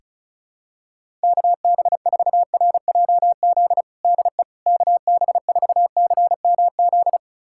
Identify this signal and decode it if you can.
Morse code, heard as a keyed tone, about 36 wpm, 700 Hz. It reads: KB4RJZ DE KB4CMZ
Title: USA Callsign DE Callsign - 6 Characters - 35wpm